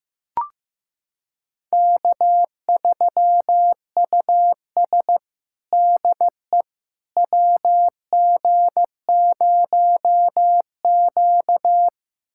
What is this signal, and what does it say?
Morse code, heard as a keyed tone, about 15 wpm, 700 Hz. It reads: K3US DE WG0Q
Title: USA Callsign DE Callsign - 4 Characters - 15wpm